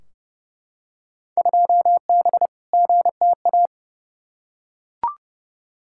Morse code, heard as a keyed tone, about 30 wpm, 700 Hz. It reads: 2B GTA
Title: Field Day Exchanges Encore - 30wpm